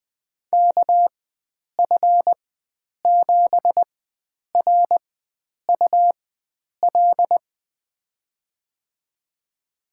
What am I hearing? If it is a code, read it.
Morse code, heard as a keyed tone, about 20 wpm, spaced as for 9 wpm, 700 Hz. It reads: KF7RUL
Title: USA Call Signs - 6 Characters - 4x Character Spacing - 20wpm